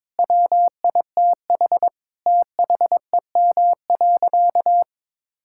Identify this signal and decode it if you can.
Morse code, heard as a keyed tone, about 21 wpm, 700 Hz. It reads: WITH THEM.
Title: Sentences from Top 1600 Words 22wpm